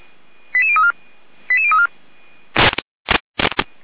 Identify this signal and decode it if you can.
Five-tone selective call (ZVEI), heard as quick runs of five tones, tones of about 70 ms. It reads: 88034 88034